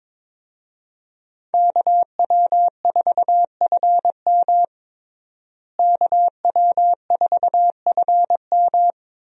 Morse code, heard as a keyed tone, about 22 wpm, 700 Hz. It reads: KW4FM KW4FM
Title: USA Call Signs - 5 Characters - Repeated 3x - 22wpm